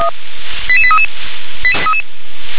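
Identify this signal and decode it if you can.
Five-tone selective call (ZVEI), heard as quick runs of five tones, tones of about 70 ms. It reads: 88833
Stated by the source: ZVEI five-tone selective call received by SDR.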